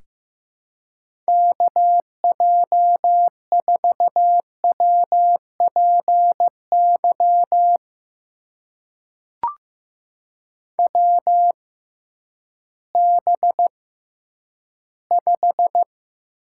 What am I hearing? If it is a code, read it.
Morse code, heard as a keyed tone, about 15 wpm, spaced as for 10 wpm, 700 Hz. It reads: KJ4WPY WB5
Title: USA Call Signs - 6 Characters - 6x Character Spacing - 15wpm